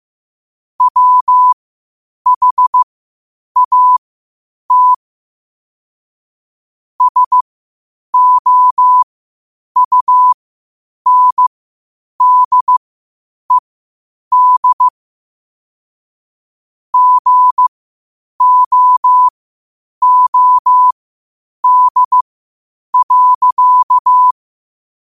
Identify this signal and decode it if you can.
Morse code, heard as a keyed tone, about 15 wpm, spaced as for 8 wpm, 1000 Hz. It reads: WHAT SOUNDED GOOD.